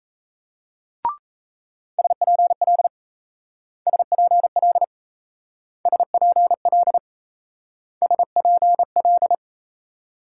Morse code, heard as a keyed tone, about 36 wpm, 700 Hz. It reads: SPL SPL SPL SPL